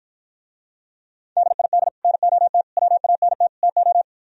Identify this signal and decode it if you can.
Morse code, heard as a keyed tone, about 54 wpm, 700 Hz. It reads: DID NOT WANT TO